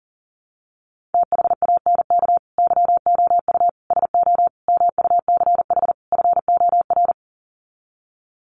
Morse code, heard as a keyed tone, about 40 wpm, 700 Hz. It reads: THANK YOU SO MUCH FOR
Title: Sets of 5 Words 40wpm